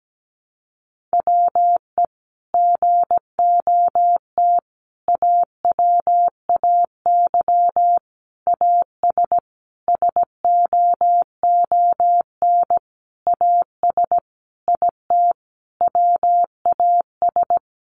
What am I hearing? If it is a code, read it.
Morse code, heard as a keyed tone, about 17 wpm, 700 Hz. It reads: WE GOT AWAY AS SOON AS IT WAS
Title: Sentences from Top 600 Words 17wpm